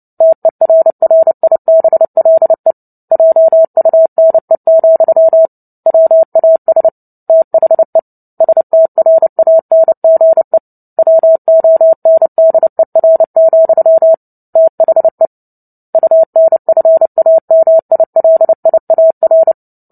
Morse code, heard as a keyed tone, about 30 wpm, 650 Hz. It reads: TERRIBLE JUNE, WAS THE STRANGE WONDER, THE UNFAMILIAR
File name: Book Two / 03-Chapter 3 - 29 WPM